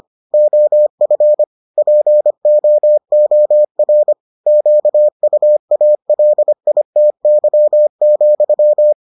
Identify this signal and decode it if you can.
Morse code, heard as a keyed tone, about 25 wpm, 600 Hz. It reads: OF POOR QUALITY,